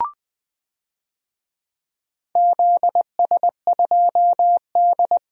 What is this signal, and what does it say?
Morse code, heard as a keyed tone, about 20 wpm, 700 Hz. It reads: ZS2D
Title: DXCC-100 Call Signs - 4 Characters - 20wpm